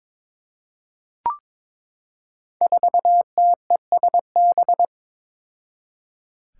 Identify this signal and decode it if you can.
Morse code, heard as a keyed tone, about 22 wpm, 700 Hz. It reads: 4TESB
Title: Five Challenging Letter-Number Combinations 22wpm